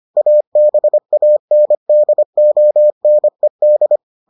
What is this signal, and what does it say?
Morse code, heard as a keyed tone, about 25 wpm, 600 Hz. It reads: ABANDONED